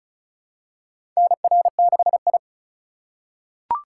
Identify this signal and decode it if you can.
Morse code, heard as a keyed tone, about 34 wpm, 700 Hz. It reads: NR6I
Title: USA Call Signs - 4 Characters - 3x Character Spacing - 35wpm